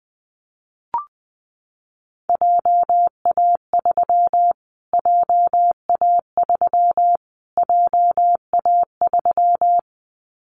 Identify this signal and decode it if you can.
Morse code, heard as a keyed tone, about 19 wpm, 700 Hz. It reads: JA3 JA3 JA3
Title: Callsign Prefixes for Top 25 DXCC - Repeated 3x - 20wpm